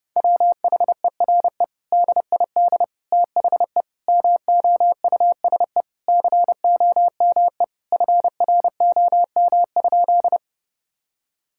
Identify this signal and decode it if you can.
Morse code, heard as a keyed tone, about 30 wpm, 700 Hz. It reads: WHERE DID THE MOUSE COME FROM?
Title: Sentences from Top 1000 Words - Review Part 5 - 30wpm